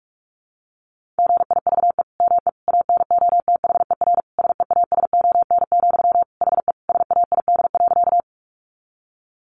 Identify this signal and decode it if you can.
Morse code, heard as a keyed tone, about 44 wpm, 700 Hz. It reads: GIVE ME ANOTHER SEASON, HE SAID.
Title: Sentences from Top 1400 Words 45wpm